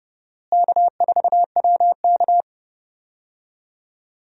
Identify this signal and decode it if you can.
Morse code, heard as a keyed tone, about 30 wpm, 700 Hz. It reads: K4WK
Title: USA Call Signs - 4 Characters - Repeated 2x - 30wpm